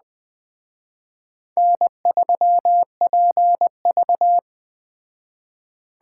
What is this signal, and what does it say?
Morse code, heard as a keyed tone, about 20 wpm, 700 Hz. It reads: N3PV